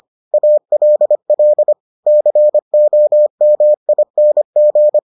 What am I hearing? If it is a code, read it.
Morse code, heard as a keyed tone, about 25 wpm, 600 Hz. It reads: ALL COMING